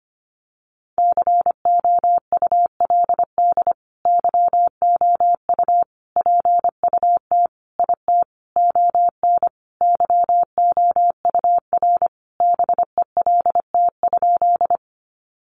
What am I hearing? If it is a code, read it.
Morse code, heard as a keyed tone, about 25 wpm, 700 Hz. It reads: COULD YOU PUT IT ON YOUR BELT?
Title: Sentences from Top 1400 Words 25wpm